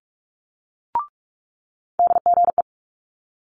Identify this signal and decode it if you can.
Morse code, heard as a keyed tone, about 46 wpm, 700 Hz. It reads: DGE
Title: Multi-letter Phonemes Quick Fire 45wpm